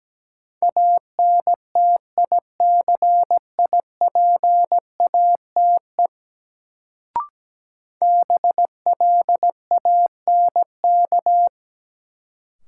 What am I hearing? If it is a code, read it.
Morse code, heard as a keyed tone, about 17 wpm, 700 Hz. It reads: ANTICIPATE BLANK
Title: Sentences from Top 1500 Words - New Word Drill - 17wpm